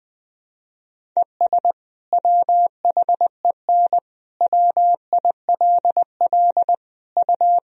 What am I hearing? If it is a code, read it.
Morse code, heard as a keyed tone, about 20 wpm, 700 Hz. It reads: ES WHEN WILL U
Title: Sentences from Top 100 Words and Abbreviations 20wpm